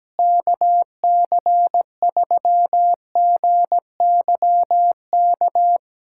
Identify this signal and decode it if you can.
Morse code, heard as a keyed tone, about 17 wpm, 700 Hz. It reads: KC3GYK
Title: USA Callsign DE Callsign 17wpm